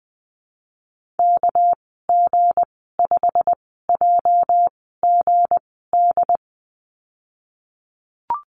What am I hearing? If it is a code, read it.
Morse code, heard as a keyed tone, about 20 wpm, spaced as for 15 wpm, 700 Hz. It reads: KG5JGD
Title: 219 Lesson 35g - Number 0 - Call Signs 20wpm